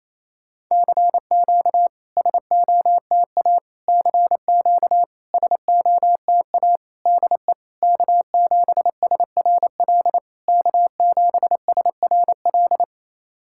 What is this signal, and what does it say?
Morse code, heard as a keyed tone, about 28 wpm, 690 Hz. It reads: CQ SOTA CQ SOTA DE K7SRL K7SRL
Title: SOTA QSO Practice 28wpm